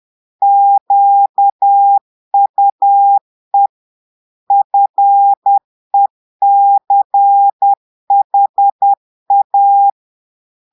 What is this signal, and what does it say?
Morse code, heard as a keyed tone, about 10 wpm, 800 Hz. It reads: QUE FECHA
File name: Portugais - texte en clair 1 - vitesse - 10 mots minute